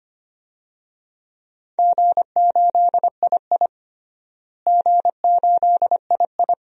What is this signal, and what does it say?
Morse code, heard as a keyed tone, about 25 wpm, 700 Hz. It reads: G8II G8II
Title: DXCC-100 Call Signs - 4 Characters - Repeated 2x - 25wpm